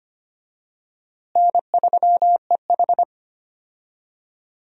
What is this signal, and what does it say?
Morse code, heard as a keyed tone, about 25 wpm, 700 Hz. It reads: N3EH
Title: USA Call Signs - 4 Characters - Repeated 3x - 25wpm